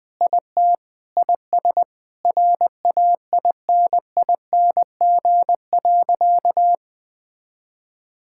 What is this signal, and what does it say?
Morse code, heard as a keyed tone, about 20 wpm, 700 Hz. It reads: IT IS RAINING.
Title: Sentences with 8-Word Phrases - 20wpm